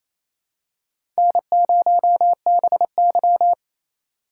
Morse code, heard as a keyed tone, about 28 wpm, 700 Hz. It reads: N0BY
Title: USA Call Signs - 8x Character Spacing - 28wpm